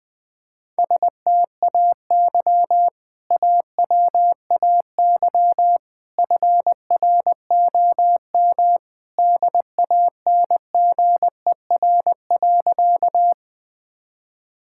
Morse code, heard as a keyed tone, about 20 wpm, 700 Hz. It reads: STAY AWAY FROM DANGER.
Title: Sentences with 2-Word Phrases - 20wpm